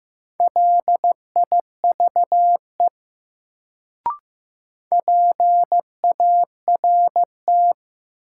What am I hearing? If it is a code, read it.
Morse code, heard as a keyed tone, about 15 wpm, 700 Hz. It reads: LIVE PART
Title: Sentences from Top 200 Words - New Word Drill - 15wpm